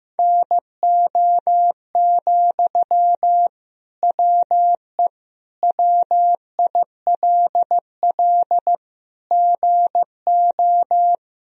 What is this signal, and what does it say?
Morse code, heard as a keyed tone, about 15 wpm, 700 Hz. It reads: NO, WE WILL GO
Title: Sentences from Top 1000 Words - Review Part 3 - 15wpm